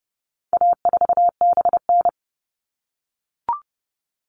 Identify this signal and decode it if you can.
Morse code, heard as a keyed tone, about 30 wpm, 700 Hz. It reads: A4BN
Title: All Cut Numbers - 4-Digit 30wpm